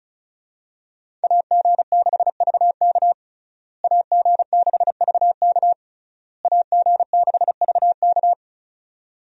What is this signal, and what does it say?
Morse code, heard as a keyed tone, about 34 wpm, 700 Hz. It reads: AG6VK AG6VK AG6VK
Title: USA Call Signs - 5 Characters - Repeated 3x - 35wpm